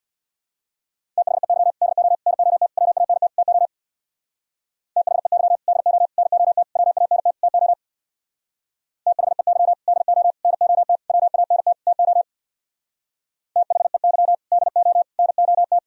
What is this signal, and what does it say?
Morse code, heard as a keyed tone, about 64 wpm, 700 Hz. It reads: THEY DO NOT WANT TO THEY DO NOT WANT TO THEY DO NOT WANT TO THEY DO NOT